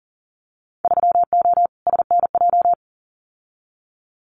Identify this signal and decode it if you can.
Morse code, heard as a keyed tone, about 40 wpm, 700 Hz. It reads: 3O SNJ
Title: Winter Field Day Exchanges Encore 40wpm